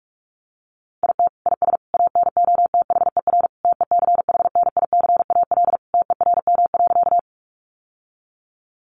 Morse code, heard as a keyed tone, about 44 wpm, 700 Hz. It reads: IT IS ANOTHER TECHNICAL TERM.
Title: Sentences from Top 1100 Words 45wpm